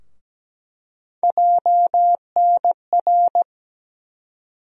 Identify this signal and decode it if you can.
Morse code, heard as a keyed tone, about 17 wpm, 700 Hz. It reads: JNR
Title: USA Call Sign Postfixes - 3 Letters 17wpm